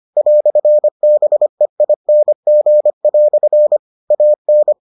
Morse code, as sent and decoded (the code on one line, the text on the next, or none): .-..-. -... .. -. --. .-..-. / .- -.
"BEING" AN